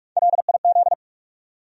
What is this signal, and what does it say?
Morse code, heard as a keyed tone, about 46 wpm, 700 Hz. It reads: LIZ
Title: USA Common Names - Max 4 Letters - 45wpm